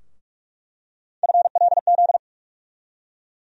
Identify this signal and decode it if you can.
Morse code, heard as a keyed tone, about 41 wpm, 700 Hz. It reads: FLZ
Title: USA Call Sign Postfixes - 3 Letters 45wpm